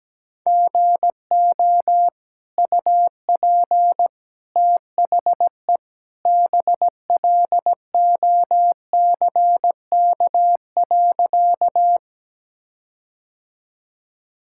Morse code, heard as a keyed tone, about 17 wpm, 700 Hz. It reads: GO UP THE BLOCK.
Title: Sentences from Top 1600 Words 17wpm